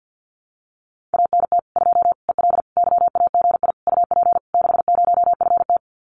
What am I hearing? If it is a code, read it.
Morse code, heard as a keyed tone, about 50 wpm, 700 Hz. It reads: ANT 2 EL YAGI UP 60FT
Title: Common QSO Phrases Encore 50wpm